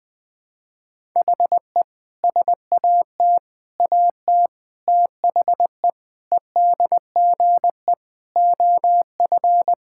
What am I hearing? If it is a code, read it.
Morse code, heard as a keyed tone, about 20 wpm, 700 Hz. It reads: HE SAT AT THE EDGE OF
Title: Sentences with 5-Word Phrases - 20wpm